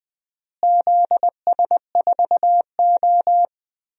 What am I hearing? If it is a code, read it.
Morse code, heard as a keyed tone, about 20 wpm, 700 Hz. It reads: ZS4O
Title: DXCC-100 Call Signs - 4 Characters - Repeated 3x - 20wpm